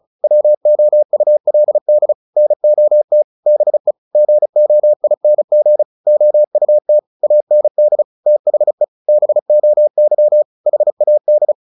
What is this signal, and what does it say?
Morse code, heard as a keyed tone, about 34 wpm, 600 Hz. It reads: WOULD NOT BE GOING OUT AND THE BOY HAD
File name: the old man and the sea 2 35wpm